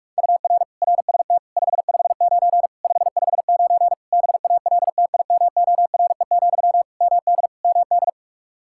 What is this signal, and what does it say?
Morse code, heard as a keyed tone, about 41 wpm, 700 Hz. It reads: UR RST 559 559 BALTIMORE, MD MD
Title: Practice QSO - Exchange 1 - Beginning - 45wpm